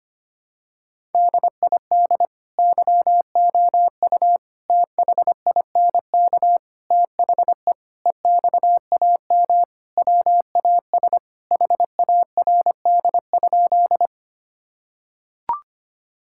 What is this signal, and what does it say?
Morse code, heard as a keyed tone, about 25 wpm, 700 Hz. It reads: DID YOU THINK THE EXAM WAS HARD?
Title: Sentences from Top 1000 Words - Review Part 5 - 25wpm